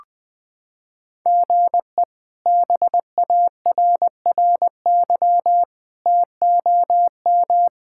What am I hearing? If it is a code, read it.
Morse code, heard as a keyed tone, about 21 wpm, 700 Hz. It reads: GE BARRY TOM
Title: K1USN SST QSO Practice - 20wpm